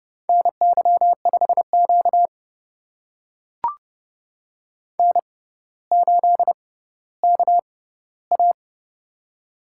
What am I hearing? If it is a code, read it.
Morse code, heard as a keyed tone, about 30 wpm, spaced as for 15 wpm, 700 Hz. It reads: NY5Q N8KA
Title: USA Call Signs - 4 Characters - 6x Character Spacing - 30wpm